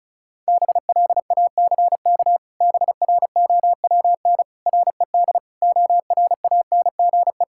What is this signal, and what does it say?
Morse code, heard as a keyed tone, about 34 wpm, 700 Hz. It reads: BLACK BROWN RED ORANGE
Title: Pangram Challenge v2 - 35wpm